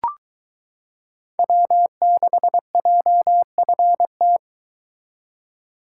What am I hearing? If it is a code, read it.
Morse code, heard as a keyed tone, about 22 wpm, 700 Hz. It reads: W6JFT